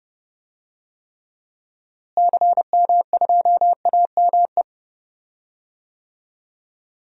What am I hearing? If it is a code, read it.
Morse code, heard as a keyed tone, about 30 wpm, 700 Hz. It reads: CM2AME
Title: DXCC-100 Call Signs - 6 Characters - Repeated 2x - 30wpm